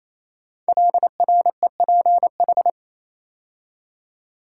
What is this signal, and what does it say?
Morse code, heard as a keyed tone, about 27 wpm, 700 Hz. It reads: LREPH
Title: Code Groups - 5 Letters - 28wpm